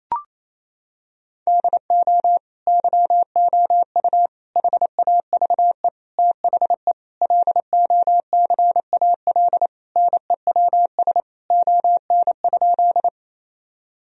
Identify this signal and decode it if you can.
Morse code, heard as a keyed tone, about 28 wpm, 700 Hz. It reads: DO YOU HAVE THE LOCAL NEWS ON?
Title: Sentences from Top 1000 Words - Review Part 3 - 28wpm